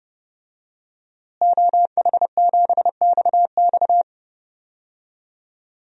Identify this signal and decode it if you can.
Morse code, heard as a keyed tone, about 30 wpm, 700 Hz. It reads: OH7XX
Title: DXCC-100 Call Signs - 5 Characters - Repeated 2x - 30wpm